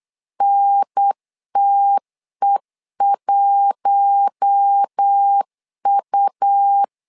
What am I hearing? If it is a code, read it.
Morse code, heard as a keyed tone, about 8 wpm, 800 Hz. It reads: NTE1U
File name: LEZIONI / 4 Q1 8wpm 3035